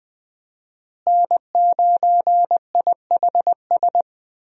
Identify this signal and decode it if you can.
Morse code, heard as a keyed tone, about 20 wpm, 700 Hz. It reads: N9IHS
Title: USA Call Signs - 5 Characters - Repeated 3x - 20wpm